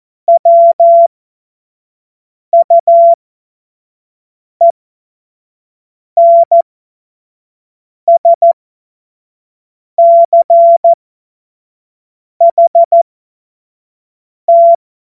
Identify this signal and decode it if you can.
Morse code, heard as a keyed tone, about 14 wpm, spaced as for 5 wpm, 680 Hz. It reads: WUENSCHT
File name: Morsekurs / Witze / Witz 26. Oktober 2021, 25 F70